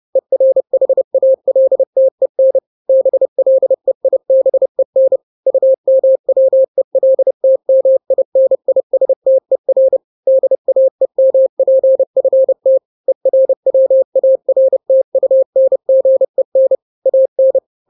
Morse code, heard as a keyed tone, about 30 wpm, 530 Hz. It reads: ERHALTEN BLEIBEN UMWELTMINISTER DAEMPFT ERWARTUNGEN AN
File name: daily / 29wpm